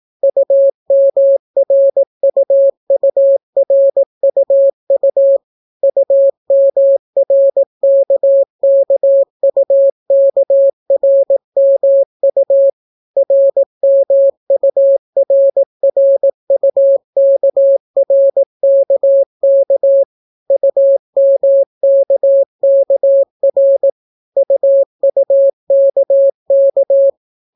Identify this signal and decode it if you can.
Morse code, heard as a keyed tone, about 18 wpm, 550 Hz. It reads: UMRUURUU UMRKKUKRMU RMURRUKRKK UMKKR UUKK